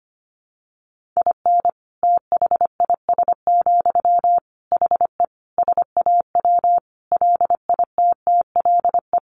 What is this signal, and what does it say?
Morse code, heard as a keyed tone, about 25 wpm, 700 Hz. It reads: IN THIS, HE SAW LITTLE
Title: Sentences from Top 1500 Words 25wpm